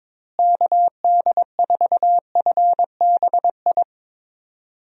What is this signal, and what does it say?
Morse code, heard as a keyed tone, about 22 wpm, 700 Hz. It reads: KD4FBI
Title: USA Call Signs - 6 Characters - Repeated 3x - 22wpm